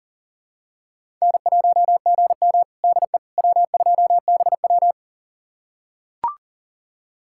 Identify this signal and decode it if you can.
Morse code, heard as a keyed tone, about 40 wpm, 700 Hz. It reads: N1GM DE W2BW
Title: USA Callsign DE Callsign - 4 Characters - 40wpm